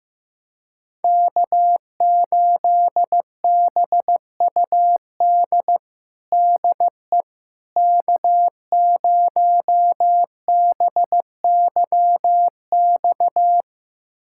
Morse code, heard as a keyed tone, about 15 wpm, 700 Hz. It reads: K8BUD DE K0BYX
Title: USA Callsign DE Callsign - 5 Characters - 15wpm